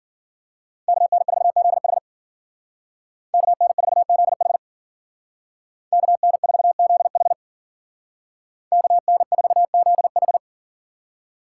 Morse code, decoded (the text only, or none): KN4ZH KN4ZH KN4ZH KN4ZH